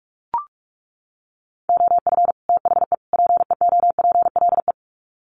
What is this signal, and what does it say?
Morse code, heard as a keyed tone, about 44 wpm, 700 Hz. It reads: OF THE PEOPLE
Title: Sets of 3 Words 45wpm